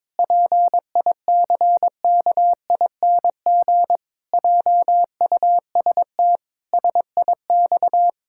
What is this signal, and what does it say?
Morse code, heard as a keyed tone, about 22 wpm, 700 Hz. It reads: PICKING JUST SIX
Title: Pangram Challenge v2 - 22wpm